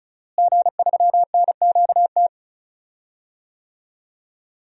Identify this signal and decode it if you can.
Morse code, heard as a keyed tone, about 36 wpm, 700 Hz. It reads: G3NQT
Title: DXCC-100 Call Signs - 5 Characters - Repeated 2x - 35wpm